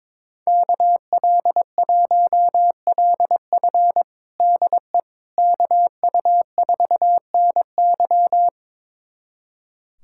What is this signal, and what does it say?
Morse code, heard as a keyed tone, about 22 wpm, 700 Hz. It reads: KL1LF DE KU4NY
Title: USA Callsign DE Callsign - 5 Characters - 22wpm